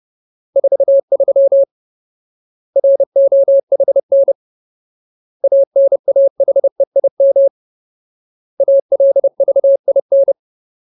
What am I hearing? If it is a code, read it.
Morse code, heard as a keyed tone, about 30 wpm, 550 Hz. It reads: 43 ROHN ANAHEIM ALVIN
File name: Ultimate Word List / 30WPM Ultimate